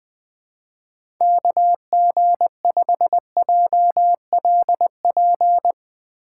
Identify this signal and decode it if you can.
Morse code, heard as a keyed tone, about 20 wpm, 700 Hz. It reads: KG5JLP
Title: USA Call Signs - 3x Character Spacing - 20wpm